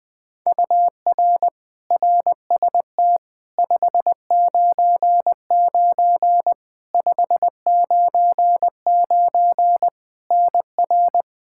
Morse code, heard as a keyed tone, about 21 wpm, 700 Hz. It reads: UR RST 599 599 NR
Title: Practice QSO - Exchange 1 - Beginning - 20wpm